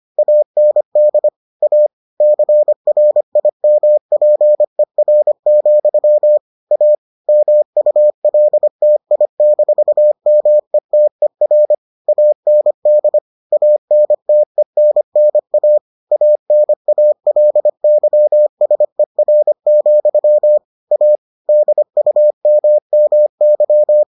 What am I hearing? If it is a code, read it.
Morse code, heard as a keyed tone, about 25 wpm, 600 Hz. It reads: AND A CRIMPER, A MULTI-METER AND ANTENNA ANALYSER, A DUMMY